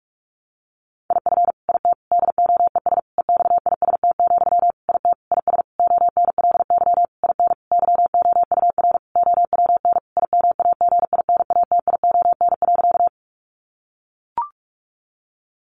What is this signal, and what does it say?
Morse code, heard as a keyed tone, about 44 wpm, 700 Hz. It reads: IF IT DOES EXIST, IT IS ONLY IN YOUR OWN IMAGINATION.
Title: Sentences from Top 1300 Words 45wpm